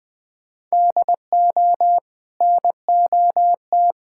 Morse code, heard as a keyed tone, about 20 wpm, 700 Hz. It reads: DO NOT
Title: Build A Phrase - 5-Word Phrases - 20wpm